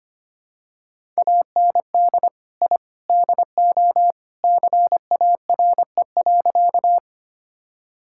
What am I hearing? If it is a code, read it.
Morse code, heard as a keyed tone, about 25 wpm, 700 Hz. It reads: AND I DO CARE.
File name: practice / 25wpm / Sentences with 3-Word Phrases - 25wpm